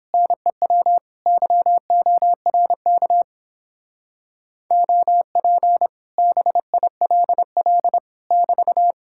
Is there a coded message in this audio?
Yes: Morse code from 0.1 to 8.9 s, heard as a keyed tone, about 29 wpm, 700 Hz. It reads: NEW YORK OP BILL <BT>